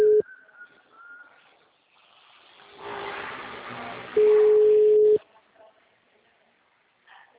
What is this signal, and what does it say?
Ringback tone (Europe, 425 Hz).
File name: out- -2001- - - .72271